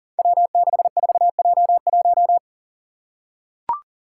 Morse code, heard as a keyed tone, about 40 wpm, 700 Hz. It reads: W64J1
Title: Five Challenging Letter-Number Combinations 40wpm